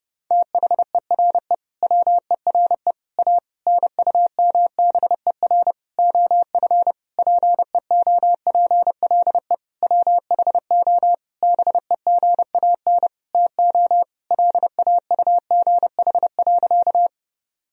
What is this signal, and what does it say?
Morse code, heard as a keyed tone, about 32 wpm, 700 Hz. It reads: THERE WERE A NUMBER OF PEOPLE WHO BEGAN TO LAUGH.
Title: Sentences from Top 1000 Words - Review Part 1 - 30wpm